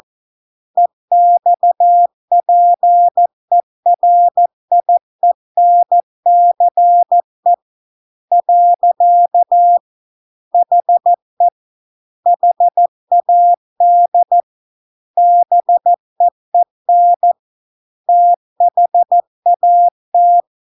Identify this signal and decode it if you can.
Morse code, heard as a keyed tone, about 14 wpm, 700 Hz. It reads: EXPERIENCE . HE HAD BEEN THAT